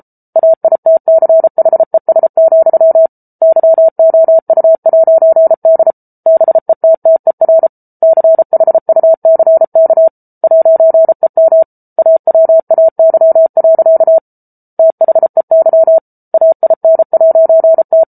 Morse code, heard as a keyed tone, about 34 wpm, 650 Hz. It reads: AITCHES, YOU'D BETTER CHUCK 'EM AWAY. THEY AIN'T